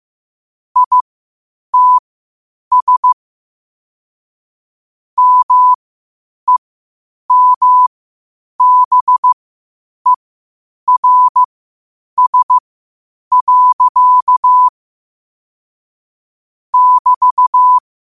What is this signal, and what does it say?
Morse code, heard as a keyed tone, about 15 wpm, spaced as for 8 wpm, 1000 Hz. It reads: ITS MEMBERS. <BT>
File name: 8wpm